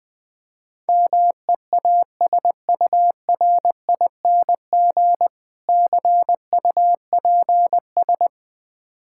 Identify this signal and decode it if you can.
Morse code, heard as a keyed tone, about 20 wpm, 700 Hz. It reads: MEASURING CUPS
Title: Easy Words - Speed-Racing with Hints - 20wpm